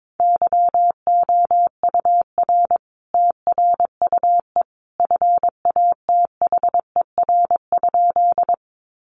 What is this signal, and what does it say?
Morse code, heard as a keyed tone, about 22 wpm, 700 Hz. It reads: YOUR TRUE FATHER?
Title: Sentences from Top 400 Words 22wpm